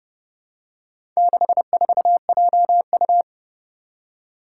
Morse code, heard as a keyed tone, about 31 wpm, 700 Hz. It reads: 64JU
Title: Four Challenging Letter-Number Combinations 30wpm